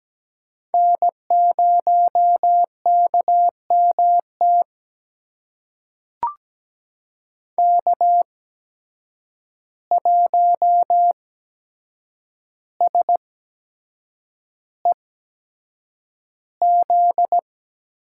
Morse code, heard as a keyed tone, about 17 wpm, spaced as for 7 wpm, 700 Hz. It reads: N0KMT K1SEZ
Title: USA Call Signs - 5 Characters - 8x Character Spacing - 17wpm